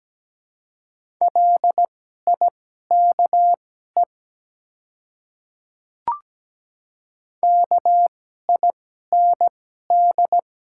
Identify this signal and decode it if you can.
Morse code, heard as a keyed tone, about 17 wpm, spaced as for 12 wpm, 700 Hz. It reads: LIKE KIND
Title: 2x Character Spacing - Top 200 Words - 17wpm